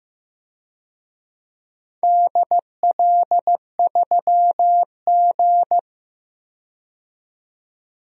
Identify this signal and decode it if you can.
Morse code, heard as a keyed tone, about 15 wpm, 700 Hz. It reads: DL3G
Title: DXCC-100 Call Signs - 4 Characters - 15wpm